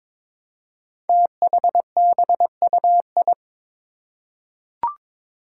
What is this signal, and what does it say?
Morse code, heard as a keyed tone, about 22 wpm, 700 Hz. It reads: THBUI
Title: Five Challenging Letter-Number Combinations 22wpm